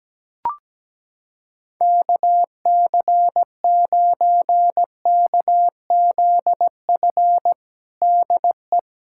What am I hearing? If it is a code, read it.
Morse code, heard as a keyed tone, about 17 wpm, 700 Hz. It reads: KC9KZF DE
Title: USA Callsign DE Callsign - 6 Characters - 17wpm